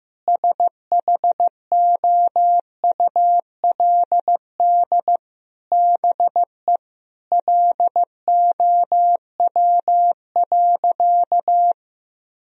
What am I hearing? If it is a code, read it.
Morse code, heard as a keyed tone, about 15 wpm, 700 Hz. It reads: SHOULD BE LOW.
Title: Sentences from Top 1000 Words - Review Part 1 - 15wpm